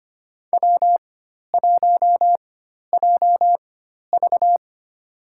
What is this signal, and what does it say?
Morse code, heard as a keyed tone, about 25 wpm, spaced as for 12 wpm, 700 Hz. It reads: W1JV
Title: USA Call Signs - 4 Characters - 4x Character Spacing - 25wpm